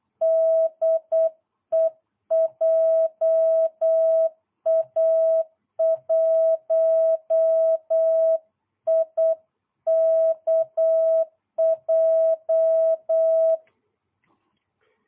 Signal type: Morse code